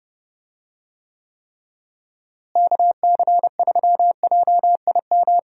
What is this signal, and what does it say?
Morse code, heard as a keyed tone, about 30 wpm, 700 Hz. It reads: KC3JIM
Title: USA Call Signs - 6 Characters - Repeated 3x - 30wpm